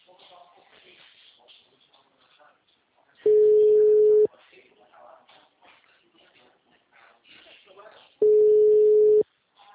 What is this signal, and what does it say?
Ringback tone (Europe, 425 Hz), with speech.